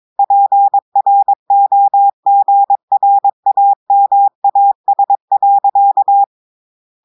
Morse code, heard as a keyed tone, about 22 wpm, 800 Hz. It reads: PROGRAMAS.
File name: audios / portugais / portugais texte en clair 2 vitesse 22 mots minutes